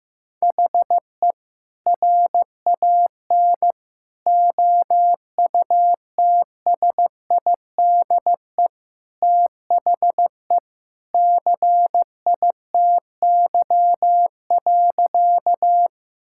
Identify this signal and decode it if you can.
Morse code, heard as a keyed tone, about 15 wpm, 700 Hz. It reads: HE RAN OUTSIDE THE CITY.
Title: Sentences from Top 1000 Words - Review Part 6 - 15wpm